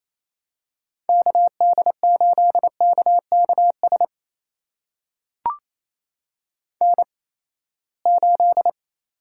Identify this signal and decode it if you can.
Morse code, heard as a keyed tone, about 28 wpm, spaced as for 19 wpm, 700 Hz. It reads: KD8KKS N8
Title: USA Call Signs - 8x Character Spacing - 28wpm